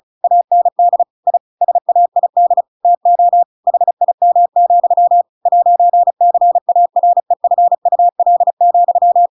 Morse code, heard as a keyed tone, about 34 wpm, 700 Hz. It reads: AND I SAID TO HIM, 'CAREFUL,
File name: Pinocchio / Pinocchio ch7 35 35 ik1hge